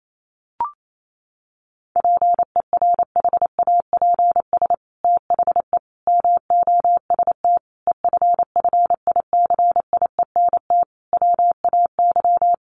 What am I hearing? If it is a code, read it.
Morse code, heard as a keyed tone, about 28 wpm, 700 Hz. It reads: PERHAPS THE MOST EFFICIENT WAY
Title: Sentences from Top 1300 Words 28wpm